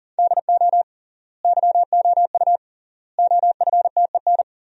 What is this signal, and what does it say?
Morse code, heard as a keyed tone, about 40 wpm, 700 Hz. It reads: DO YOU OFTEN
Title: 3x Word-Spacing - Sentences from Top 300 Words - 40wpm